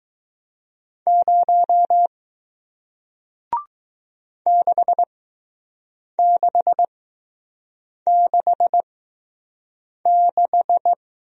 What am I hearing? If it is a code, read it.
Morse code, heard as a keyed tone, about 20 wpm, spaced as for 6 wpm, 700 Hz. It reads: 0 6666